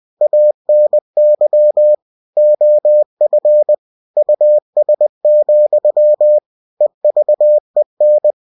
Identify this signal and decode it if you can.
Morse code, heard as a keyed tone, about 20 wpm, 600 Hz. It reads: ANY OF US, EVEN
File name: cw-books / winnie-the-pooh / winnie-20wpm-page-0063